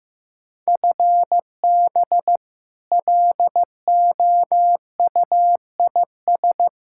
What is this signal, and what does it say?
Morse code, heard as a keyed tone, about 15 wpm, 700 Hz. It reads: FB LOUIS
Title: K1USN SST QSO Practice 15wpm